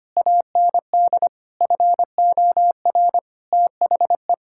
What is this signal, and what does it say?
Morse code, heard as a keyed tone, about 25 wpm, 700 Hz. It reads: AND FOR THE
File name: practice / 25wpm / Sentences with 9-Word Phrases - 25wpm